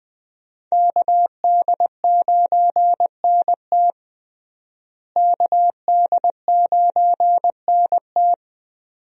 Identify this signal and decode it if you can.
Morse code, heard as a keyed tone, about 20 wpm, 700 Hz. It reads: KD9NT KD9NT
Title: USA Call Signs - 5 Characters - Repeated 2x - 20wpm